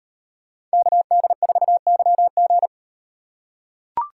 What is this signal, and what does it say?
Morse code, heard as a keyed tone, about 34 wpm, 700 Hz. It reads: KD4YG